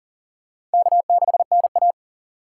Morse code, heard as a keyed tone, about 40 wpm, 700 Hz. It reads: K6NA
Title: K1USN SST QSO Practice - 40wpm